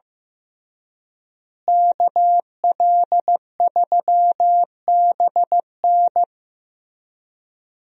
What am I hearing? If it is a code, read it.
Morse code, heard as a keyed tone, about 15 wpm, 700 Hz. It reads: KL3BN
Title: USA Call Signs - 5 Characters - Repeated 3x - 15wpm